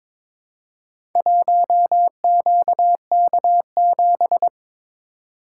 Morse code, heard as a keyed tone, about 22 wpm, 700 Hz. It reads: 1QK7
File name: practice / 22wpm / Four Letter-Number Combinations - 22wpm